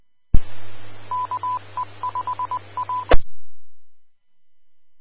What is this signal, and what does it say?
Morse code, heard as a keyed tone, about 20 wpm, 1000 Hz. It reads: KE5A
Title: Austin, TX Amateur Repeaters